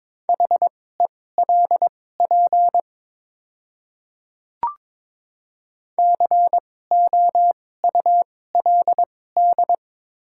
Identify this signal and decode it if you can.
Morse code, heard as a keyed tone, about 22 wpm, spaced as for 16 wpm, 700 Hz. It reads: HELP COULD
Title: 2x Character Spacing - Top 200 Words - 22wpm